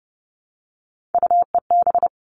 Morse code, heard as a keyed tone, about 30 wpm, 700 Hz. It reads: UEB
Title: All Cut Numbers - 3-Digit 30wpm